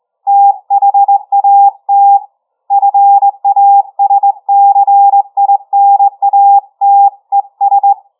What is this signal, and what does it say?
Morse code, heard as a keyed tone, about 22 wpm, 800 Hz. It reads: THAT FASCINATES
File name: challenge-133